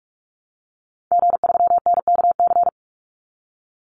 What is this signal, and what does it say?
Morse code, heard as a keyed tone, about 44 wpm, 700 Hz. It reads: G3NKC
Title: Call Signs Repeated 2x - 45wpm